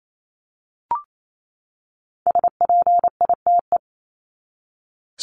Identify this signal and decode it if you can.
Morse code, heard as a keyed tone, about 28 wpm, 700 Hz. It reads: SPITE
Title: Top 1751-2000 Words 28wpm